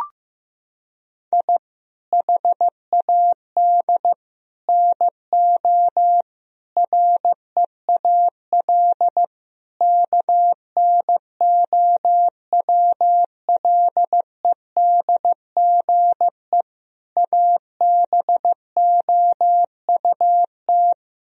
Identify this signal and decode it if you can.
Morse code, heard as a keyed tone, about 15 wpm, 700 Hz. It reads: I HAD NO REAL KNOWLEDGE ABOUT
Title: Sentences from Top 1000 Words - Review Part 3 - 15wpm